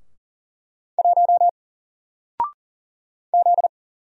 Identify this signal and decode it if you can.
Morse code, heard as a keyed tone, about 39 wpm, 700 Hz. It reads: 1 Z
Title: Single Letter-Number 40wpm